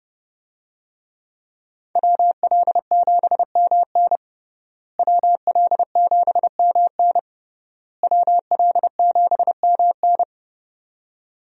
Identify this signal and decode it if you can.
Morse code, heard as a keyed tone, about 31 wpm, 700 Hz. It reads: WL7MN WL7MN WL7MN
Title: USA Call Signs - 5 Characters - Repeated 3x - 30wpm